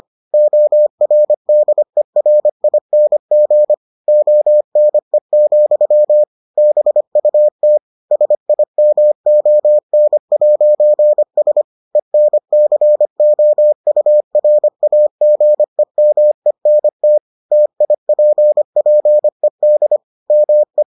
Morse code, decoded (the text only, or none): ORDERING ONE, BUT SIMON'S ENCOURAGEMENT TIPPED ME